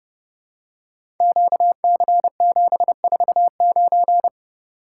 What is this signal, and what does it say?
Morse code, heard as a keyed tone, about 30 wpm, 700 Hz. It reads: QC749
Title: Code Groups - 5 Letter:Numbers - 30wpm